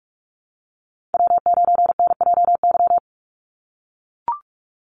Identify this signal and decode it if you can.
Morse code, heard as a keyed tone, about 44 wpm, 700 Hz. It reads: W9NJY
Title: Call Signs - 45wpm